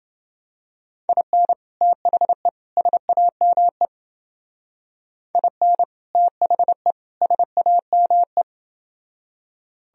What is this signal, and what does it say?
Morse code, heard as a keyed tone, about 29 wpm, 700 Hz. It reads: IN THE SAME IN THE SAME